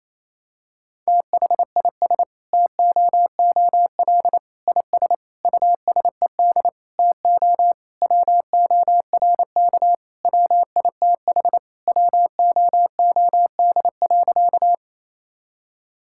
Morse code, heard as a keyed tone, about 27 wpm, 700 Hz. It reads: THIS TOOL IS USED TO WORK WITH WOOD.
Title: Sentences from Top 1000 Words - Review Part 6 - 28wpm